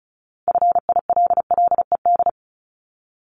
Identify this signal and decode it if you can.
Morse code, heard as a keyed tone, about 36 wpm, 700 Hz. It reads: FILLED
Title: Multi-letter Phonemes 35wpm